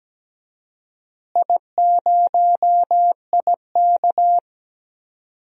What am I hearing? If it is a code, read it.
Morse code, heard as a keyed tone, about 17 wpm, 700 Hz. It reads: I0IK
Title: DXCC-100 Call Signs - 4 Characters - Repeated 3x - 17wpm